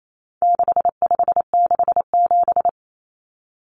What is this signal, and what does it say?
Morse code, heard as a keyed tone, about 28 wpm, 700 Hz. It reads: 6567
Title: Numbers - 4-Digit 28wpm